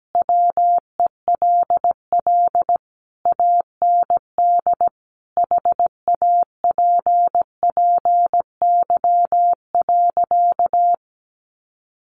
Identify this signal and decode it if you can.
Morse code, heard as a keyed tone, about 17 wpm, 700 Hz. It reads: WELL AND HAPPY.
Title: Sentences from Top 1000 Words 17wpm